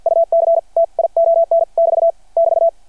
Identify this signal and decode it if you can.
Morse code, heard as a keyed tone, about 53 wpm, spaced as for 42 wpm, 660 Hz. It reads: UCTION <BT> <BT>
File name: archived / CRSnet